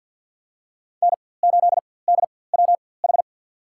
Morse code, heard as a keyed tone, about 53 wpm, spaced as for 24 wpm, 700 Hz. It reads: N8DWH
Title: USA Call Signs - 4x Character Spacing - 50wpm